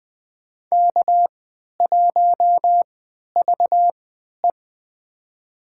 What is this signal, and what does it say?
Morse code, heard as a keyed tone, about 20 wpm, spaced as for 11 wpm, 700 Hz. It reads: K1VE
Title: USA Call Signs - 3x Character Spacing - 20wpm